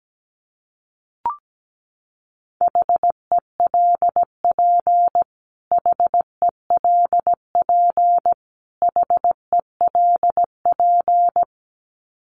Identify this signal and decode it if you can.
Morse code, heard as a keyed tone, about 17 wpm, 700 Hz. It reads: HELP HELP HELP
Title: Top 500 Words Repeated 17wpm